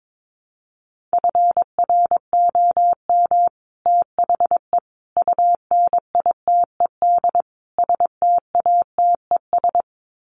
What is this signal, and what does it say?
Morse code, heard as a keyed tone, about 22 wpm, 700 Hz. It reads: FROM THE UNITED STATES
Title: Sets of 4 Words 22wpm